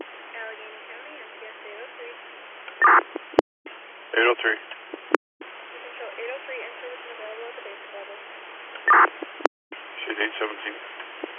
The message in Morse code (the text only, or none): none